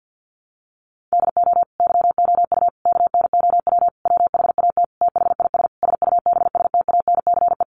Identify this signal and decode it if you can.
Morse code, heard as a keyed tone, about 50 wpm, 700 Hz. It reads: DO YOU KNOW WHAT THIS SUBSTANCE
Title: Sentences from Top 1500 Words 50wpm